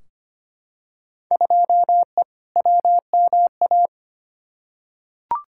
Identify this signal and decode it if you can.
Morse code, heard as a keyed tone, about 25 wpm, 700 Hz. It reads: 2E WMA
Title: Field Day Exchanges Encore - 25wpm